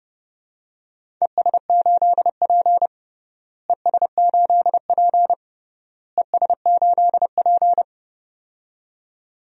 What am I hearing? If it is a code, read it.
Morse code, heard as a keyed tone, about 30 wpm, 700 Hz. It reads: ES8P ES8P ES8P
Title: DXCC-100 Call Signs - 4 Characters - Repeated 3x - 30wpm